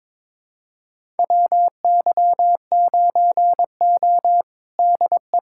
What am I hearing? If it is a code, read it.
Morse code, heard as a keyed tone, about 21 wpm, 700 Hz. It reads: WY9O DE
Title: USA Callsign DE Callsign - 4 Characters - 22wpm